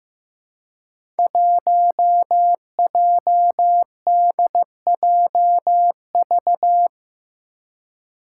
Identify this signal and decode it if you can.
Morse code, heard as a keyed tone, about 15 wpm, 700 Hz. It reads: 1JDJV
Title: Five Challenging Letter-Number Combinations 15wpm